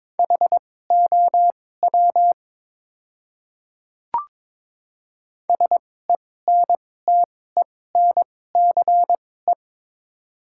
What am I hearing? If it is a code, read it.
Morse code, heard as a keyed tone, about 22 wpm, spaced as for 16 wpm, 700 Hz. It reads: HOW SENTENCE
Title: 2x Character Spacing - Top 300 Words - 22wpm